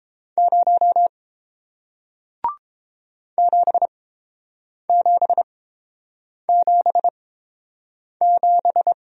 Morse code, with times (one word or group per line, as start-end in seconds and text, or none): 0.4-1.1 0
3.4-8.9 7777